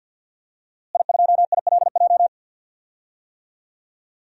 Morse code, heard as a keyed tone, about 53 wpm, 700 Hz. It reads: I2ILJ
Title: DXCC-100 Call Signs - 5 Characters - Repeated 2x - 50wpm